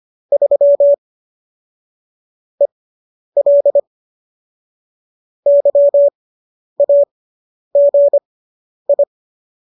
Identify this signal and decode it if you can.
Morse code, heard as a keyed tone, about 25 wpm, spaced as for 10 wpm, 570 Hz. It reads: 3 EL YAGI